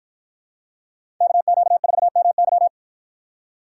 Morse code, heard as a keyed tone, about 53 wpm, 700 Hz. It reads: KQ4MY